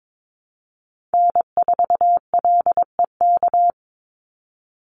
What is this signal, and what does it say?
Morse code, heard as a keyed tone, about 22 wpm, 700 Hz. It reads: N4LEK
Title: USA Call Signs 22wpm